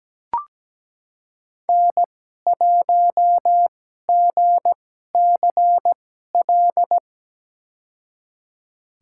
Morse code, heard as a keyed tone, about 17 wpm, spaced as for 12 wpm, 700 Hz. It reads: N1GCL
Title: USA Call Signs - 5 Characters - 2x Character Spacing - 17wpm